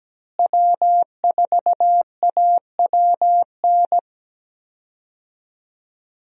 Morse code, heard as a keyed tone, about 17 wpm, 700 Hz. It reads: W4AWN
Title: USA Call Signs - 5 Characters - Repeated 3x - 17wpm